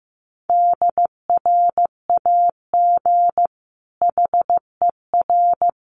Morse code, heard as a keyed tone, about 15 wpm, 700 Hz. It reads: DRAG HER
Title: Sentences from Top 1500 Words 15wpm